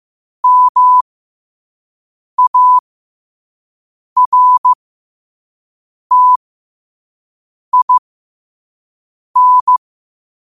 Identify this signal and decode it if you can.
Morse code, heard as a keyed tone, about 15 wpm, spaced as for 5 wpm, 1000 Hz. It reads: MARTIN
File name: qotdfiles / 5wpm